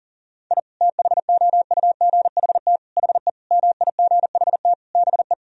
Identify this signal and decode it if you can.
Morse code, heard as a keyed tone, about 37 wpm, 700 Hz. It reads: I THOUGHT HE MIGHT BE
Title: Sentences from Top 1000 Words - Review Part 4 - 40wpm